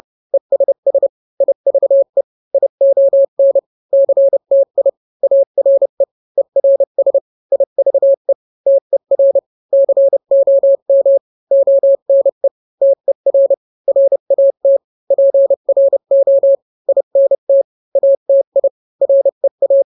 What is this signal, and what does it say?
Morse code, heard as a keyed tone, about 30 wpm, 550 Hz. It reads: ESS IVE ION CTI ARE ERS IVE TER COM ONE TER RAT PRO INT ATI REA